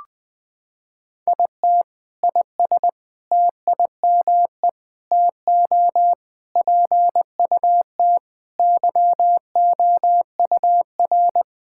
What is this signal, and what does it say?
Morse code, heard as a keyed tone, about 20 wpm, 700 Hz. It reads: IT IS TIME TO PUT YOUR
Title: Sentences with 7-Word Phrases - 20wpm